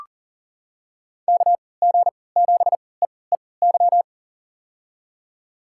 Morse code, heard as a keyed tone, about 40 wpm, spaced as for 23 wpm, 700 Hz. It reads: KG7EEY
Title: USA Call Signs - 6 Characters - 3x Character Spacing - 40wpm